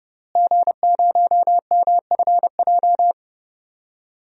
Morse code, heard as a keyed tone, about 30 wpm, 700 Hz. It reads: G0MFJ
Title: DXCC-100 Call Signs - 5 Characters - Repeated 2x - 30wpm